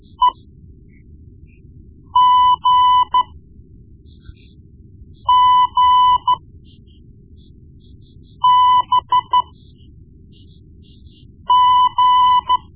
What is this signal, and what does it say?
Morse code, heard as a keyed tone, about 13 wpm, spaced as for 4 wpm, 1000 Hz. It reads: EGGBG
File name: Morse Lessons / Morse Lessons / Tape-1B